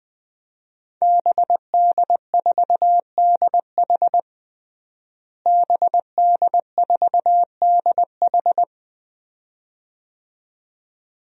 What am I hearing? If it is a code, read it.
Morse code, heard as a keyed tone, about 20 wpm, 700 Hz. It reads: BD4DH BD4DH
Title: DXCC-100 Call Signs - 5 Characters - Repeated 2x - 20wpm